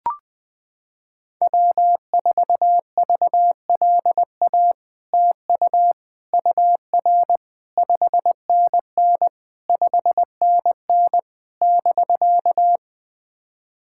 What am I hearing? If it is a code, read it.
Morse code, heard as a keyed tone, about 20 wpm, 700 Hz. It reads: W4VLA TU UR 5NN 5NN <BK>
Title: SOTA QSO Practice 20wpm